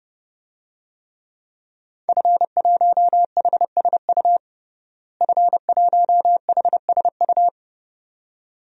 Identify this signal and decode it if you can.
Morse code, heard as a keyed tone, about 32 wpm, 700 Hz. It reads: F1HSU F1HSU
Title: DXCC-100 Call Signs - 5 Characters - Repeated 2x - 30wpm